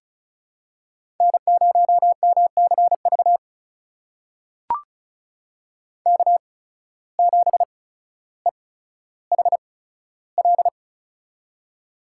Morse code, heard as a keyed tone, about 36 wpm, spaced as for 15 wpm, 700 Hz. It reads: N0MCV K7EHL
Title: USA Call Signs - 5 Characters - 8x Character Spacing - 35wpm